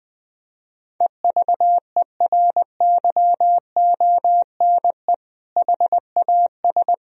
Morse code, heard as a keyed tone, about 21 wpm, 700 Hz. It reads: EVERYONE HAS
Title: Sentences from Top 1000 Words - Review Part 4 - 20wpm